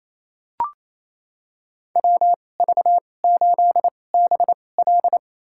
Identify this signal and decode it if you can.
Morse code, heard as a keyed tone, about 28 wpm, spaced as for 20 wpm, 700 Hz. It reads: WV8BL
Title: USA Call Signs - 5 Characters - 2x Character Spacing - 28wpm